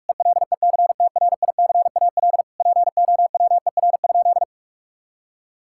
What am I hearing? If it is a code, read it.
Morse code, heard as a keyed tone, about 41 wpm, 700 Hz. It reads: ELECTRICAL POWER?
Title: Sentences from Top 1000 Words - Review Part 6 - 45wpm